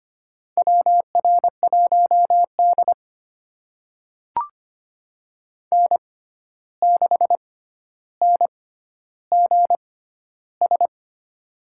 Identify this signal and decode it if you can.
Morse code, heard as a keyed tone, about 25 wpm, spaced as for 12 wpm, 700 Hz. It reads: WR1D N6NGS
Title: USA Call Signs - 6x Character Spacing - 25wpm